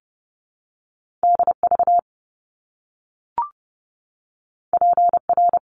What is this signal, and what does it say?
Morse code, heard as a keyed tone, about 30 wpm, 700 Hz. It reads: DV PR
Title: Top Two Letter Combinations in Words 30wpm